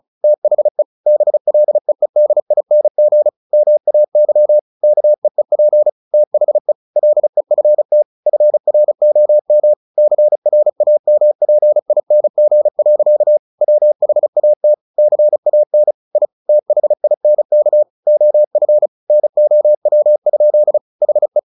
Morse code, heard as a keyed tone, about 34 wpm, 600 Hz. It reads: THE BLEEDING MAY KEEP THE LEFT FROM CRAMPING. WHAT CAN I THINK OF NOW? HE
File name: the old man and the sea 2 35wpm